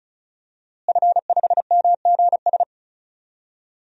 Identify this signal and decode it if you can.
Morse code, heard as a keyed tone, about 36 wpm, 700 Hz. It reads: F5MGS
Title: DXCC-100 Call Signs - 5 Characters - Repeated 3x - 35wpm